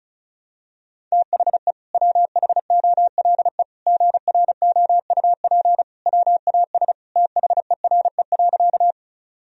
Morse code, heard as a keyed tone, about 34 wpm, 700 Hz. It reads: THE WHOLE GROUP WAS THERE.
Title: Sentences from Top 1000 Words - Review Part 1 - 35wpm